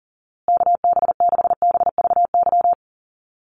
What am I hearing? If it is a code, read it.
Morse code, heard as a keyed tone, about 40 wpm, 700 Hz. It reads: KB6BVY
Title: USA Call Signs Repeated 2x 40wpm